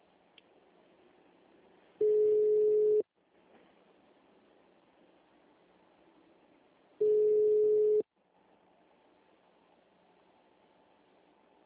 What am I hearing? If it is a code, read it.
Ringback tone (Europe, 425 Hz).
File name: out- -2004- - - .62